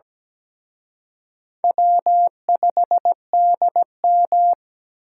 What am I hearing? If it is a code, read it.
Morse code, heard as a keyed tone, about 20 wpm, 700 Hz. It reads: W5DM
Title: Speed Racing - USA Call Signs 17wpm